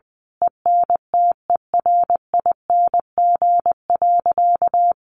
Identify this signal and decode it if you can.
Morse code, heard as a keyed tone, about 19 wpm, 700 Hz. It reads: ENTERING.
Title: A Sound Of Thunder 20wpm